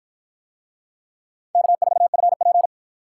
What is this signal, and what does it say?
Morse code, heard as a keyed tone, about 50 wpm, 700 Hz. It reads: K4FP